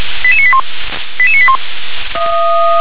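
Five-tone selective call (ZVEI), heard as quick runs of five tones, tones of about 70 ms. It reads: 88981 88981
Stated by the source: ZVEI five-tone call received by SDR.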